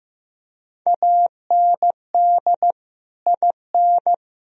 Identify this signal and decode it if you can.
Morse code, heard as a keyed tone, about 15 wpm, 700 Hz. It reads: AND IN
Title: Sentences from Top 1000 Words - Review Part 2 - 15wpm